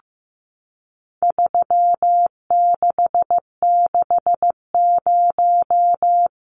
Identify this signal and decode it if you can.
Morse code, heard as a keyed tone, about 15 wpm, 700 Hz. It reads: 3660
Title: Numbers - 4-Digit 15wpm